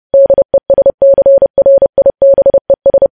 Morse code, heard as a keyed tone, about 30 wpm, 550 Hz. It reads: DESCRIBES
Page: news headlines in morse code at 30 wpm